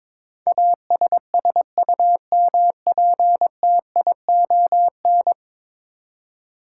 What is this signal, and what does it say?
Morse code, heard as a keyed tone, about 22 wpm, 700 Hz. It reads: ASSUMPTION
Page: morse-code-ninja-practice-22wpm